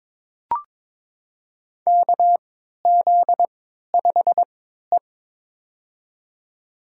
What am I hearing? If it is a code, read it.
Morse code, heard as a keyed tone, about 22 wpm, spaced as for 13 wpm, 700 Hz. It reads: KZ5E
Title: USA Call Signs - 3x Character Spacing - 22wpm